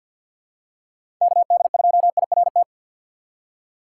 Morse code, heard as a keyed tone, about 47 wpm, 700 Hz. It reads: KD2IRT
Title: USA Call Signs - 6 Characters - 3x Character Spacing - 50wpm